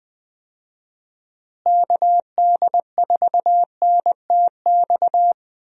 Morse code, heard as a keyed tone, about 21 wpm, 700 Hz. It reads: KD4NTX
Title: USA Call Signs - 6 Characters - Repeated 2x - 20wpm